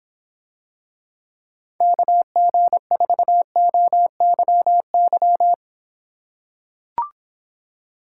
Morse code, heard as a keyed tone, about 26 wpm, 700 Hz. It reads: KG4OYY